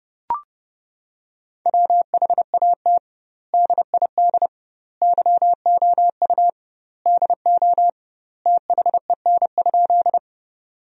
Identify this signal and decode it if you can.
Morse code, heard as a keyed tone, about 32 wpm, 700 Hz. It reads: WHAT DID YOU DO THEN?
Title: 2x Word-Spacing - Sentences from Top 100 Words - 30wpm